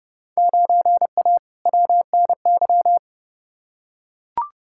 Morse code, heard as a keyed tone, about 30 wpm, 700 Hz. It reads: 9A WNY
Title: Field Day Exchanges Encore - 30wpm